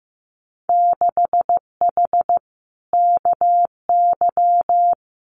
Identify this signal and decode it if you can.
Morse code, heard as a keyed tone, about 15 wpm, 700 Hz. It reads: 6H KY
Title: Winter Field Day Exchanges Encore 15wpm